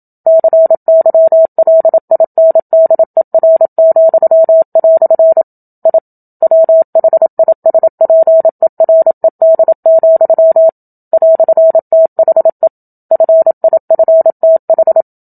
Morse code, heard as a keyed tone, about 27 wpm, 650 Hz. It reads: CYLINDER," I WHISPERED, "THE FIFTH
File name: Book Two / 01-Chapter 1 - 27 WPM